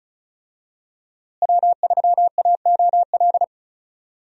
Morse code, heard as a keyed tone, about 36 wpm, 700 Hz. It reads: W3AOL
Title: USA Call Signs - 5 Characters - Repeated 2x - 35wpm